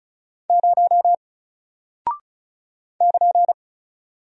Morse code, heard as a keyed tone, about 36 wpm, 700 Hz. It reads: 0 <KN>
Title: Single Character (Letter:Number:Punctuation:Prosign) Alt 35wpm